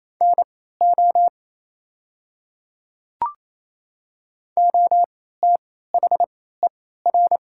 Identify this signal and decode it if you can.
Morse code, heard as a keyed tone, about 28 wpm, spaced as for 16 wpm, 690 Hz. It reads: NO OTHER
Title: 3x Character Spacing - Top 100 Words - 28wpm